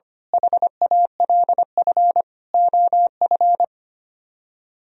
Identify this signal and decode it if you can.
Morse code, heard as a keyed tone, about 24 wpm, 700 Hz. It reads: HALF OF
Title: Speed-Racing - Sets of 5 Words 25wpm